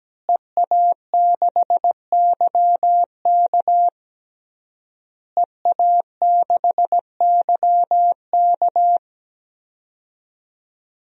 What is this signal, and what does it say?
Morse code, heard as a keyed tone, about 17 wpm, 700 Hz. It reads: EA6YK EA6YK
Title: DXCC-100 Call Signs - 5 Characters - Repeated 3x - 17wpm